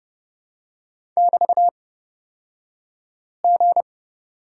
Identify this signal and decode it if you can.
Morse code, heard as a keyed tone, about 31 wpm, spaced as for 5 wpm, 700 Hz. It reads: <BT>G
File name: Lesson 36d - BT - Two Characters 30wpm